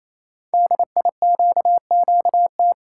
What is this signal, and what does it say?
Morse code, heard as a keyed tone, about 28 wpm, 700 Hz. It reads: DIQQT
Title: Code Groups - 5 Letters - 28wpm